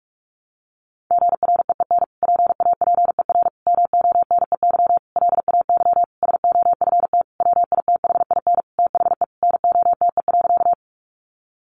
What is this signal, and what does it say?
Morse code, heard as a keyed tone, about 44 wpm, 700 Hz. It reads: GREEN PAPER MONEY LAY SOFT WITHIN THE NOTE.
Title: Sentences from Top 1500 Words 45wpm